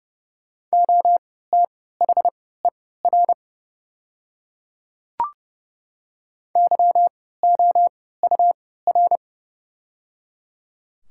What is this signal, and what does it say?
Morse code, heard as a keyed tone, about 29 wpm, spaced as for 17 wpm, 700 Hz. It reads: OTHER YOUR
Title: 3x Character Spacing - Top 200 Words - 30wpm